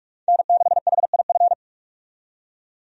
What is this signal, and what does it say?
Morse code, heard as a keyed tone, about 46 wpm, 700 Hz. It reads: N6HIF
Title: USA Call Signs - 5 Characters - 45wpm